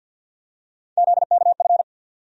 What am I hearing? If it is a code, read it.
Morse code, heard as a keyed tone, about 53 wpm, 700 Hz. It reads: ZKF
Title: USA Call Sign Postfixes - 3 Letters 50wpm